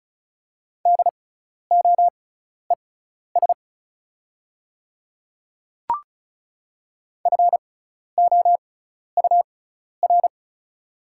Morse code, heard as a keyed tone, about 34 wpm, spaced as for 12 wpm, 690 Hz. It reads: DOES FOUR
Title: 6x Character Spacing - Top 300 Words - 35wpm